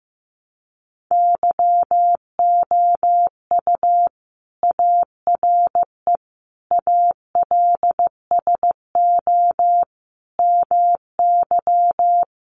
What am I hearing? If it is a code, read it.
Morse code, heard as a keyed tone, about 15 wpm, 700 Hz. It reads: YOU ARE ALSO MY
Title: Sentences from Top 1300 Words 15wpm